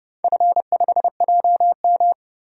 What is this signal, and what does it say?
Morse code, heard as a keyed tone, about 30 wpm, 700 Hz. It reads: F5JM
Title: DXCC-100 Call Signs - 4 Characters - 30wpm